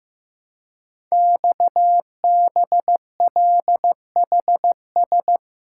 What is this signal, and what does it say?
Morse code, heard as a keyed tone, about 15 wpm, 700 Hz. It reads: XBLHS
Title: Code Groups - 5 Letters - 15wpm